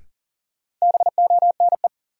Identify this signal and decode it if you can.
Morse code, heard as a keyed tone, about 40 wpm, 700 Hz. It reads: BONE
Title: Sentences from Top 1600 Words - New Word Drill - 40wpm